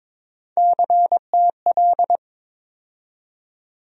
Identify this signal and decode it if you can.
Morse code, heard as a keyed tone, about 22 wpm, 700 Hz. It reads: CTL
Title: USA Call Sign Postfixes - 3 Letters 22wpm